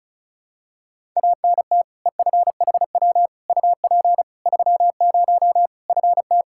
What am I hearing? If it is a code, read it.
Morse code, heard as a keyed tone, about 36 wpm, 700 Hz. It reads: ANT EFHW UP 30 FT
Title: Practice QSO - Exchange 2 - Info - 35wpm